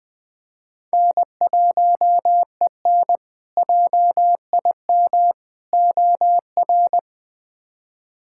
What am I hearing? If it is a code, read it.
Morse code, heard as a keyed tone, about 20 wpm, 700 Hz. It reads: N1EN JIM OR
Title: K1USN SST QSO Practice - 20wpm